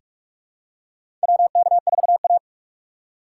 Morse code, heard as a keyed tone, about 46 wpm, 700 Hz. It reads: WK4A
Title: USA Call Signs - 4 Characters - 2x Character Spacing - 45wpm